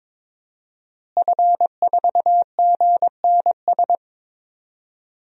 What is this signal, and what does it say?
Morse code, heard as a keyed tone, about 23 wpm, 700 Hz. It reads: F4GNS
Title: DXCC-100 Call Signs - 5 Characters - Repeated 3x - 22wpm